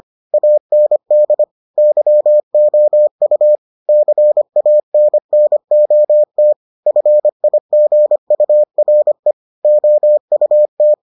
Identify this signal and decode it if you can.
Morse code, heard as a keyed tone, about 25 wpm, 600 Hz. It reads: AND YOU CANNOT FIGURE OUT